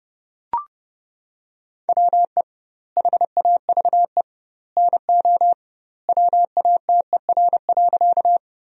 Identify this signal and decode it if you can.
Morse code, heard as a keyed tone, about 30 wpm, 700 Hz. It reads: WE HAVE NO WATER.
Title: 2x Word-Spacing - Sentences from Top 100 Words - 30wpm